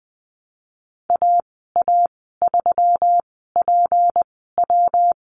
Morse code, heard as a keyed tone, about 20 wpm, spaced as for 15 wpm, 700 Hz. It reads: AA3PW
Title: 154 Lesson 25g - Letter p - Call Signs 20wpm